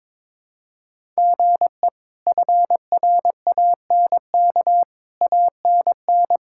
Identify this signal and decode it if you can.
Morse code, heard as a keyed tone, about 21 wpm, 700 Hz. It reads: GE FRANK ANN
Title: K1USN SST QSO Practice - 22wpm